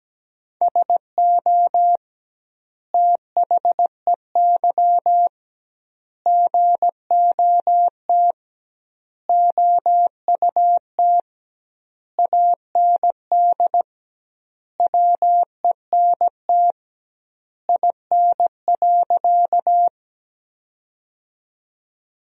Morse code, heard as a keyed tone, about 17 wpm, 700 Hz. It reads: SO THEY GOT OUT AND WENT IN.
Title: 2x Word-Spacing - Sentences from Top 300 Words - 17wpm